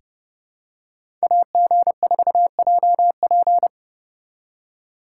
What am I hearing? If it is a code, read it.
Morse code, heard as a keyed tone, about 32 wpm, 700 Hz. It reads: AG4JP
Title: USA Call Signs - 5 Characters - 2x Character Spacing - 30wpm